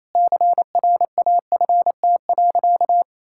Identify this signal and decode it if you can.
Morse code, heard as a keyed tone, about 27 wpm, 700 Hz. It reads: CRAFT.
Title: Sentences from Top 1000 Words - Review Part 6 - 28wpm